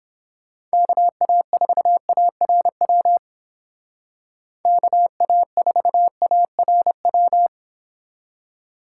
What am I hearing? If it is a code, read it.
Morse code, heard as a keyed tone, about 29 wpm, 700 Hz. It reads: KA4ARW KA4ARW